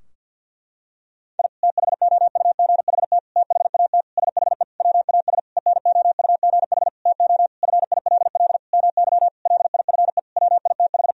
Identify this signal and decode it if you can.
Morse code, heard as a keyed tone, about 52 wpm, 700 Hz. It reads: I THOUGHT THAT SHE WAS ENOUGH TO FILL MY LIFE WITH
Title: Sentences from Top 1000 Words - Review Part 3 - 50wpm